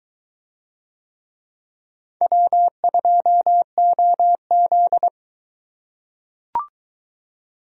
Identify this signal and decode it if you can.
Morse code, heard as a keyed tone, about 21 wpm, 700 Hz. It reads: W2OZ